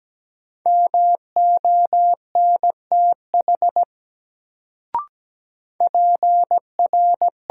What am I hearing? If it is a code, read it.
Morse code, heard as a keyed tone, about 17 wpm, 700 Hz. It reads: MONTH PR
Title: Sentences from Top 800 Words - New Word Drill - 17wpm